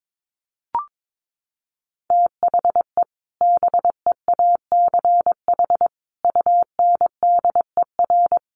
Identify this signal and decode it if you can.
Morse code, heard as a keyed tone, about 22 wpm, 700 Hz. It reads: THE BEACH UNDER
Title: Sentences from Top 1500 Words 22wpm